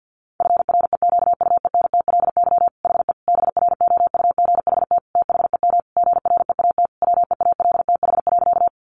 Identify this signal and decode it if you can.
Morse code, heard as a keyed tone, about 50 wpm, 700 Hz. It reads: FREQUENTLY HE BROUGHT THEM GREAT WEALTH.
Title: Sentences from Top 1300 Words 50wpm